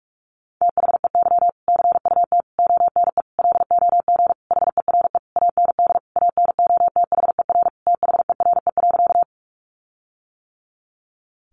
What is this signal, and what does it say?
Morse code, heard as a keyed tone, about 44 wpm, 700 Hz. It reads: THEY CUT ONE LOG HERE AND ANOTHER THERE.
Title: Sentences from Top 1400 Words 45wpm